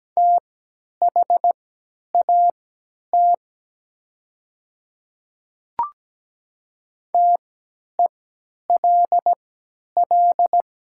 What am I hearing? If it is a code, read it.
Morse code, heard as a keyed tone, about 17 wpm, spaced as for 10 wpm, 700 Hz. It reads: THAT TELL
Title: 3x Character Spacing - Top 200 Words - 17wpm